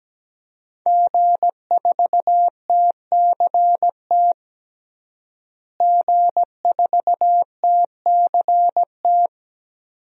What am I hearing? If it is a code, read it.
Morse code, heard as a keyed tone, about 17 wpm, 700 Hz. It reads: G4TCT G4TCT
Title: DXCC-100 Call Signs - 5 Characters - Repeated 2x - 17wpm